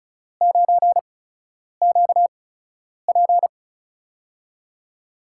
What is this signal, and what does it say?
Morse code, heard as a keyed tone, about 34 wpm, spaced as for 10 wpm, 700 Hz. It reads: 9QP
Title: USA Call Signs - 5 Characters - 8x Character Spacing - 35wpm